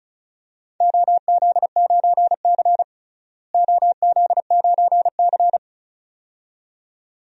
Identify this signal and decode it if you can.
Morse code, heard as a keyed tone, about 36 wpm, 700 Hz. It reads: OZ9C OZ9C
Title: DXCC-100 Call Signs - 4 Characters - Repeated 2x - 35wpm